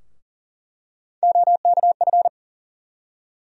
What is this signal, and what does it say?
Morse code, heard as a keyed tone, about 40 wpm, 700 Hz. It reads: OKF
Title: USA Call Sign Postfixes - 3 Letters 40wpm